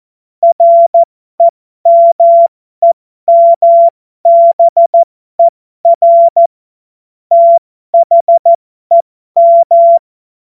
Morse code, heard as a keyed tone, about 14 wpm, spaced as for 12 wpm, 680 Hz. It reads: REMEMBER THEM
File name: Morsekurs / Witze